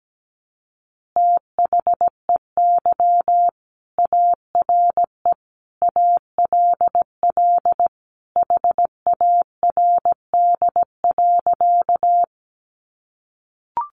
Morse code, heard as a keyed tone, about 17 wpm, 700 Hz. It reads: THEY ARE ALL HARD.
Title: Additional Easy Phrases 17wpm